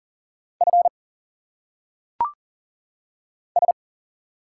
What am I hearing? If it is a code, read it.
Morse code, heard as a keyed tone, about 40 wpm, 700 Hz. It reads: F S
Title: Single Letters - Phonetic Pronunciation 40wpm